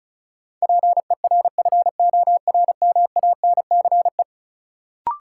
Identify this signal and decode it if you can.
Morse code, heard as a keyed tone, about 36 wpm, 700 Hz. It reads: PERFORMANCE
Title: Sentences from Top 1000 Words - New Word Drill - 35wpm